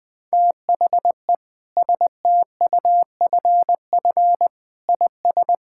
Morse code, heard as a keyed tone, about 21 wpm, 700 Hz. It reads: THE STUFF IS
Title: Sentences from Top 1600 Words 20wpm